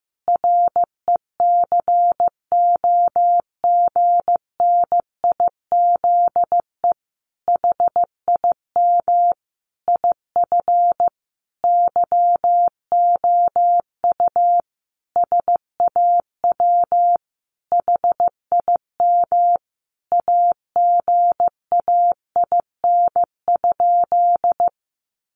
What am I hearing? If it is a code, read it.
Morse code, heard as a keyed tone, about 15 wpm, 700 Hz. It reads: RECOGNIZE HIM IF YOU SAW HIM AGAIN?
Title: Sentences from Top 1400 Words 15wpm